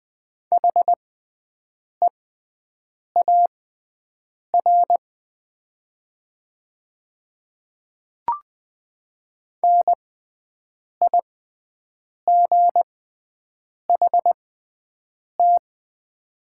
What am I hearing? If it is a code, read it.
Morse code, heard as a keyed tone, about 20 wpm, spaced as for 7 wpm, 700 Hz. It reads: HEAR NIGHT
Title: 6x Character Spacing - Top 500 Words - 20wpm